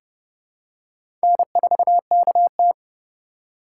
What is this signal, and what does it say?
Morse code, heard as a keyed tone, about 30 wpm, 700 Hz. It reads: N4KT
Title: USA Call Signs - 4 Characters - 2x Character Spacing - 30wpm